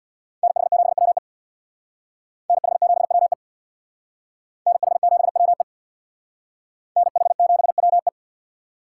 Morse code, heard as a keyed tone, about 61 wpm, 700 Hz. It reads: NH7WE NH7WE NH7WE NH7WE
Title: Speed Racing - USA 2x2 Call Signs 50wpm